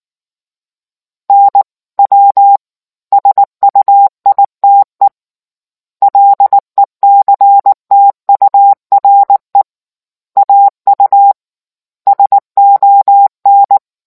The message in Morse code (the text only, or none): NW SUITE LECTURE AU SON